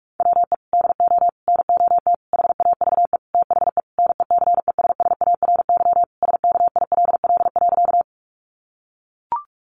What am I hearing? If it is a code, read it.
Morse code, heard as a keyed tone, about 44 wpm, 700 Hz. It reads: WE DO NOT HAVE THE NECESSARY SKILL.
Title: Sentences from Top 1200 Words 45wpm